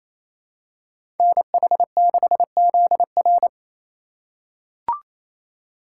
Morse code, heard as a keyed tone, about 28 wpm, 700 Hz. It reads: NH6ZR
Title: USA Call Signs - 5 Characters - 2x Character Spacing - 28wpm